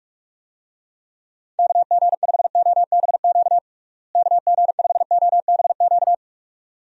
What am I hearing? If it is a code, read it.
Morse code, heard as a keyed tone, about 46 wpm, 700 Hz. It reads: KG5OBQ KG5OBQ
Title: USA Call Signs - 6 Characters - Repeated 2x - 45wpm